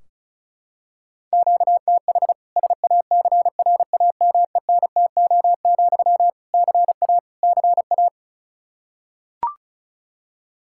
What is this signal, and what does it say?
Morse code, heard as a keyed tone, about 36 wpm, 700 Hz. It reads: QTH SACRAMENTO, CA CA
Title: K1USN SST QSO Practice 35wpm